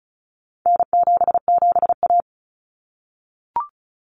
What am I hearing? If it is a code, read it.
Morse code, heard as a keyed tone, about 34 wpm, 700 Hz. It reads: N77A
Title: Common Cut Numbers - 4-Digit 35wpm